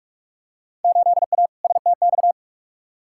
Morse code, heard as a keyed tone, about 46 wpm, 700 Hz. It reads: 8A STX
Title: Field Day Exchanges Encore - 45wpm